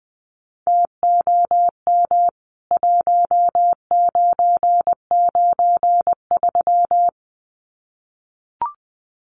Morse code, heard as a keyed tone, about 19 wpm, 700 Hz. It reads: TOM 1993
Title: CWT Exchange 20wpm